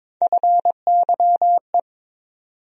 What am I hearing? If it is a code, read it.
Morse code, heard as a keyed tone, about 21 wpm, 700 Hz. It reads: FYE
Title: USA Call Sign Postfixes - 3 Letters 22wpm